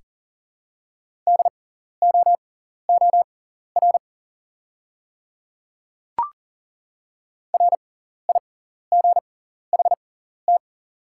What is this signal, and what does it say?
Morse code, heard as a keyed tone, about 38 wpm, spaced as for 14 wpm, 700 Hz. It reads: DOOR RIGHT
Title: 6x Character Spacing - Top 500 Words - 40wpm